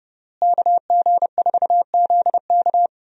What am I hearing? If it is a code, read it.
Morse code, heard as a keyed tone, about 30 wpm, 700 Hz. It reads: KG4ZK
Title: USA Call Signs - 5 Characters - Repeated 3x - 30wpm